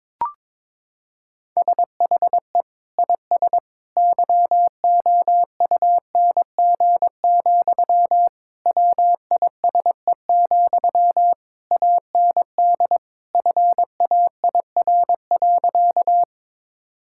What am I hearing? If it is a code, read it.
Morse code, heard as a keyed tone, about 22 wpm, 700 Hz. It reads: SHE IS YOUNG, WISE, AND FAIR.
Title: Sentences from Top 1600 Words 22wpm